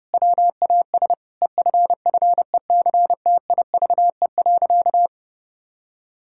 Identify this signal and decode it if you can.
Morse code, heard as a keyed tone, about 32 wpm, 700 Hz. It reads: WAS EFFECTIVE.
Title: Sentences from Top 1000 Words - Review Part 6 - 30wpm